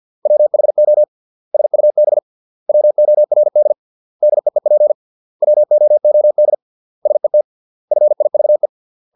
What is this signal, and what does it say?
Morse code, heard as a keyed tone, about 50 wpm, 600 Hz. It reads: WHO SUB WORD DEEP WOOD SET FIVE